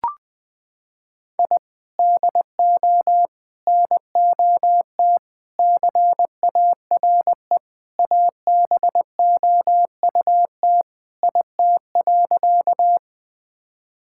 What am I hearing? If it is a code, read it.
Morse code, heard as a keyed tone, about 20 wpm, 700 Hz. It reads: I DO NOT CARE ABOUT IT.
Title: Sentences with 4-Word Phrases - 20wpm